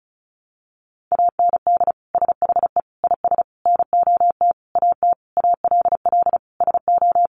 Morse code, heard as a keyed tone, about 34 wpm, 700 Hz. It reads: AND SHE IS NOT AT ALL SO
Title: Sentences from Top 600 Words 35wpm